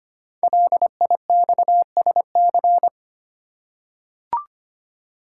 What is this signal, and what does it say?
Morse code, heard as a keyed tone, about 25 wpm, 700 Hz. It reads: LIXSC
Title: Code Groups - 5 Letters - 25wpm